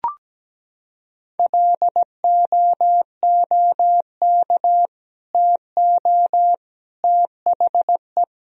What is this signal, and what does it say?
Morse code, heard as a keyed tone, about 17 wpm, 700 Hz. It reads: LOOK TO THE
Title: Sentences from Top 1000 Words - Review Part 3 - 17wpm